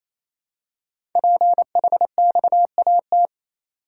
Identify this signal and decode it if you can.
Morse code, heard as a keyed tone, about 28 wpm, 700 Hz. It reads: PHXAT
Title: Code Groups - 5 Letter:Numbers - 28wpm